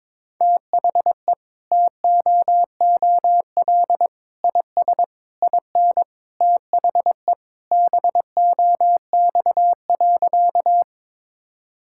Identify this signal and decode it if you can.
Morse code, heard as a keyed tone, about 21 wpm, 700 Hz. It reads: THE TOOL IS IN THE BOX.
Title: Sentences from Top 1000 Words - Review Part 6 - 22wpm